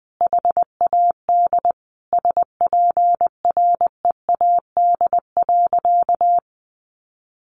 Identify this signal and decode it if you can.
Morse code, heard as a keyed tone, about 20 wpm, 700 Hz. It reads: HAD SPREAD.
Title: Sentences from Top 1200 Words 20wpm